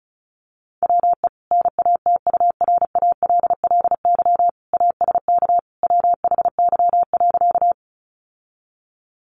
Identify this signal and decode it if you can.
Morse code, heard as a keyed tone, about 34 wpm, 700 Hz. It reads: WE NATURALLY ASK WHY.
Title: Sentences from Top 1300 Words 35wpm